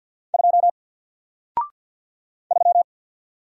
Morse code, heard as a keyed tone, about 50 wpm, 700 Hz. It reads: 2 3
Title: Single Letter-Number 50wpm